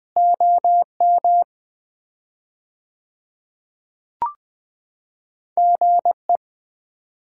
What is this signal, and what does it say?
Morse code, heard as a keyed tone, about 20 wpm, 700 Hz. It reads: OM GE
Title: Common QSO Phrases - 4x Word-Spacing - 20wpm